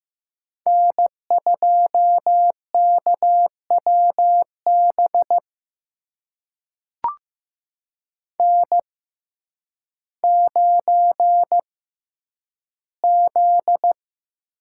Morse code, heard as a keyed tone, about 16 wpm, spaced as for 9 wpm, 700 Hz. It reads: N2KWB N9Z
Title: USA Call Signs - 5 Characters - 6x Character Spacing - 15wpm